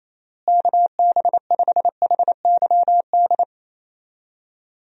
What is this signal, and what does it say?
Morse code, heard as a keyed tone, about 28 wpm, 700 Hz. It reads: KB5HYD
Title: USA Call Signs - 6 Characters - Repeated 2x - 28wpm